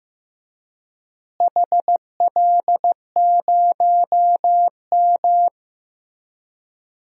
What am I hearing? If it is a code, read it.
Morse code, heard as a keyed tone, about 15 wpm, 700 Hz. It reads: HL0M
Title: DXCC-100 Call Signs - 4 Characters - Repeated 3x - 15wpm